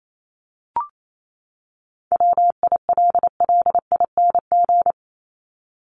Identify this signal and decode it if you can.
Morse code, heard as a keyed tone, about 28 wpm, 700 Hz. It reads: WILLING
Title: Top 751-1000 Words 28wpm